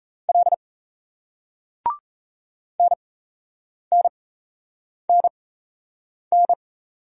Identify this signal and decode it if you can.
Morse code, heard as a keyed tone, about 35 wpm, spaced as for 8 wpm, 700 Hz. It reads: L NNNN
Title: Speed-Racing Single Letters 28wpm